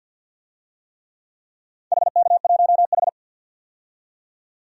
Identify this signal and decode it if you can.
Morse code, heard as a keyed tone, about 46 wpm, 700 Hz. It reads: HK1H
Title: DXCC-100 Call Signs - 4 Characters - 50wpm